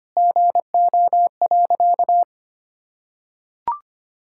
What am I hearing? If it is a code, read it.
Morse code, heard as a keyed tone, about 25 wpm, 700 Hz. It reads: GO.
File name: practice / 25wpm / Sentences with 4-Word Phrases - 25wpm